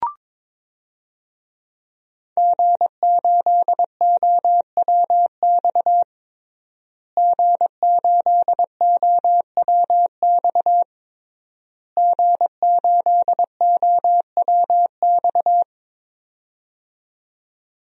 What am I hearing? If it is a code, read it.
Morse code, heard as a keyed tone, about 22 wpm, 700 Hz. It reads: G8OWX G8OWX G8OWX
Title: DXCC-100 Call Signs - 5 Characters - Repeated 3x - 22wpm